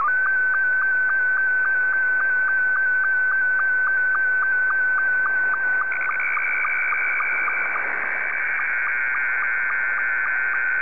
SSTV (slow-scan television), Scottie 2.
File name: sstv / S2-VK3HJV